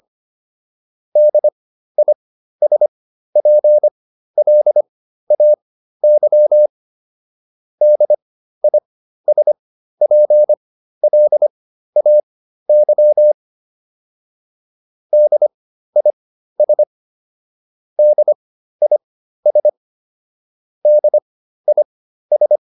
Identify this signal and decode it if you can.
Morse code, heard as a keyed tone, about 25 wpm, spaced as for 13 wpm, 600 Hz. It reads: DISPLAY DISPLAY DIS DIS DIS
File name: DIS7 13